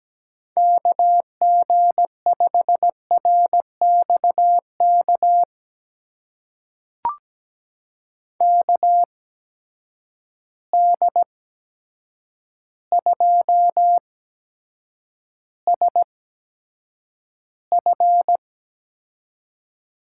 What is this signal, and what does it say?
Morse code, heard as a keyed tone, about 17 wpm, spaced as for 8 wpm, 700 Hz. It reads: KG5RXK KD2SF
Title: USA Call Signs - 6 Characters - 8x Character Spacing - 17wpm